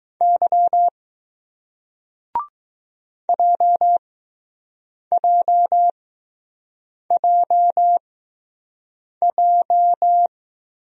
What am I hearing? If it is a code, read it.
Morse code, heard as a keyed tone, about 20 wpm, spaced as for 6 wpm, 700 Hz. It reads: Y JJJJ